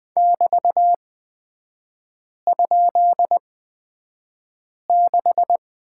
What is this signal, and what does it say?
Morse code, heard as a keyed tone, about 20 wpm, spaced as for 5 wpm, 700 Hz. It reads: <BT>?6
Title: 236 Lesson 38e - Letter q - Three Characters 20wpm